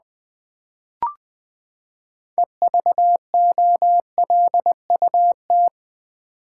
Morse code, heard as a keyed tone, about 20 wpm, 700 Hz. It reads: EVOLUT
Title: Build A Word - 6 Letters At A Time - 12-Letter Words - 20wpm